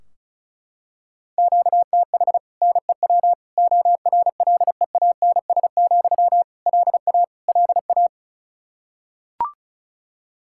Morse code, heard as a keyed tone, about 36 wpm, 700 Hz. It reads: QTH NEW ORLEANS, LA LA
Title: Practice QSO - Exchange 1 - Beginning - 35wpm